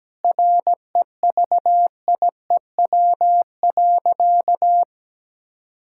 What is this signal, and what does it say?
Morse code, heard as a keyed tone, about 17 wpm, 700 Hz. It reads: REVIEW.
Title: Sentences from Top 1000 Words - Review Part 6 - 17wpm